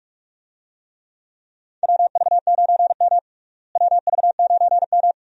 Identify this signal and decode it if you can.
Morse code, heard as a keyed tone, about 47 wpm, 700 Hz. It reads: WV9M WV9M
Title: USA Call Signs - 4 Characters - Repeated 2x - 45wpm